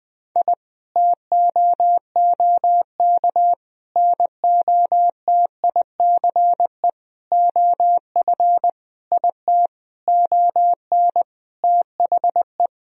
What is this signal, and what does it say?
Morse code, heard as a keyed tone, about 21 wpm, 700 Hz. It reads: I TOOK NOTICE OF IT ON THE
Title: Sentences from Top 1000 Words - Review Part 2 - 20wpm